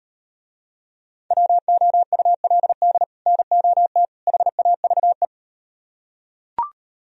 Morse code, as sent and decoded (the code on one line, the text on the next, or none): .-- --- ..- .-.. -.. / -. --- - / .... .- ...- .
WOULD NOT HAVE